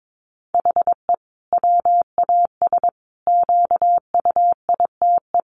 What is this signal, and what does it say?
Morse code, heard as a keyed tone, about 22 wpm, 700 Hz. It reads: HE WAS QUITE
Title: Sentences from Top 1000 Words 22wpm